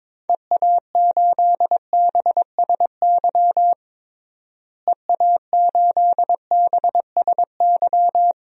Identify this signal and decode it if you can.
Morse code, heard as a keyed tone, about 21 wpm, 700 Hz. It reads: EA8BSY EA8BSY
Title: DXCC-100 Call Signs - 6 Characters - Repeated 2x - 22wpm